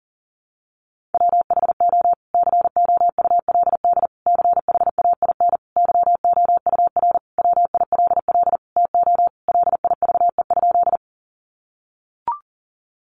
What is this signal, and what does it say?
Morse code, heard as a keyed tone, about 40 wpm, 700 Hz. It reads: WHO COULD CHAIN YOUR WILL TO LIVE?
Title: Sentences from Top 1500 Words 40wpm